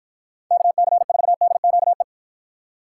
Morse code, heard as a keyed tone, about 49 wpm, 700 Hz. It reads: KC4DQE